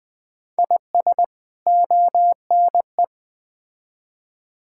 Morse code, heard as a keyed tone, about 20 wpm, 700 Hz. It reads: IS ONE
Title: Build A Phrase - 4-Word Phrases - 20wpm